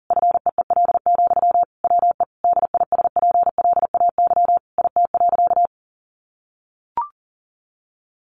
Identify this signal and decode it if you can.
Morse code, heard as a keyed tone, about 40 wpm, 700 Hz. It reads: FEEL, WE DISPLAY IT.
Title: Sentences from Top 1200 Words 40wpm